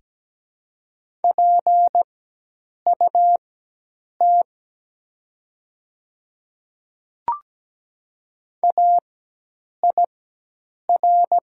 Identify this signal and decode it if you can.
Morse code, heard as a keyed tone, about 17 wpm, spaced as for 8 wpm, 700 Hz. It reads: PUT AIR
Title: 4x Character Spacing - Top 200 Words - 17wpm